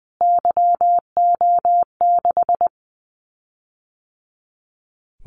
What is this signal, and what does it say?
Morse code, heard as a keyed tone, about 20 wpm, 700 Hz. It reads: YO6
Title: Callsign Prefixes for Top 25 DXCC - Repeated 3x - 20wpm